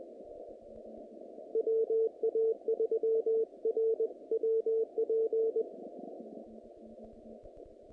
Morse code, heard as a keyed tone, about 20 wpm, 450 Hz. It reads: WA3RWP
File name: contests / naqp-cw-k5tr-op-k5pi- -2300